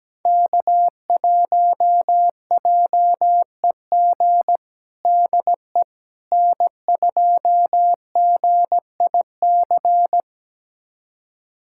Morse code, heard as a keyed tone, about 17 wpm, 700 Hz. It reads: K1JEG DE N2GIC
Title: USA Callsign DE Callsign 17wpm